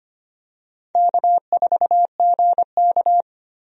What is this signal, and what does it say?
Morse code, heard as a keyed tone, about 25 wpm, 700 Hz. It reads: K4GK
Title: USA Call Signs - 4 Characters - Repeated 3x - 25wpm